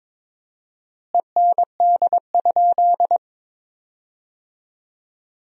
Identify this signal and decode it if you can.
Morse code, heard as a keyed tone, about 23 wpm, 700 Hz. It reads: END?
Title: 4x Word-Spacing - Sentences from Top 200 Words - 22wpm